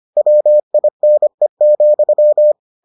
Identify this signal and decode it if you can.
Morse code, heard as a keyed tone, about 25 wpm, 600 Hz. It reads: WINE,